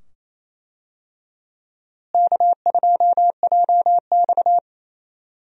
Morse code, heard as a keyed tone, about 28 wpm, 700 Hz. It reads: K2JX
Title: USA Call Signs - 4 Characters - Repeated 2x - 28wpm